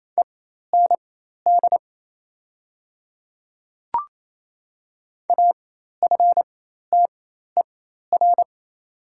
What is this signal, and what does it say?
Morse code, heard as a keyed tone, about 27 wpm, spaced as for 13 wpm, 690 Hz. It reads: END AFTER
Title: 4x Character Spacing - Top 500 Words - 28wpm